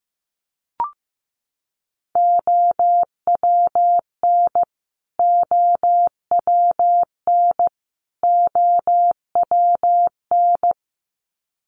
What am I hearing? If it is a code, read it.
Morse code, heard as a keyed tone, about 15 wpm, 700 Hz. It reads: OWN OWN OWN
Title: Top 300 Words Repeated 15wpm